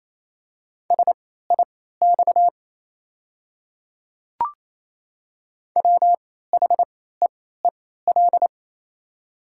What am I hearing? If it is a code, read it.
Morse code, heard as a keyed tone, about 28 wpm, spaced as for 16 wpm, 700 Hz. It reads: SIX WHEEL
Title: 3x Character Spacing - Top 500 Words - 28wpm